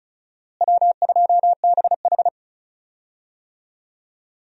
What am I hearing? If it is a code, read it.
Morse code, heard as a keyed tone, about 34 wpm, 700 Hz. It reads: W2BH
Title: USA Call Signs - 4 Characters - Repeated 3x - 35wpm